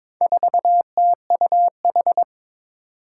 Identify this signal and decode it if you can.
Morse code, heard as a keyed tone, about 22 wpm, 700 Hz. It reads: 4TUH
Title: Four Challenging Letter-Number Combinations 22wpm